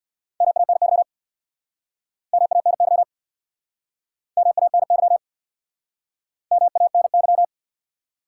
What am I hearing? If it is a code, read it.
Morse code, heard as a keyed tone, about 69 wpm, 700 Hz. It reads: MANY MANY MANY MANY